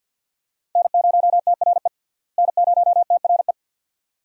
Morse code, heard as a keyed tone, about 47 wpm, 700 Hz. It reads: N0TRE N0TRE
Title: USA Call Signs - 5 Characters - Repeated 2x - 50wpm